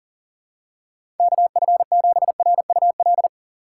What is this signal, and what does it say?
Morse code, heard as a keyed tone, about 40 wpm, 700 Hz. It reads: KF7RUL
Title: USA Call Signs - 6 Characters - 3x Character Spacing - 40wpm